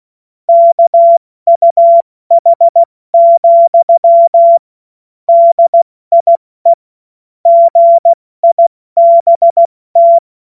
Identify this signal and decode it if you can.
Morse code, heard as a keyed tone, about 16 wpm, 680 Hz. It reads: KUH, DIE GIBT
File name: Morsekurs / Witze / Witz 01. Februar 2022 70 F80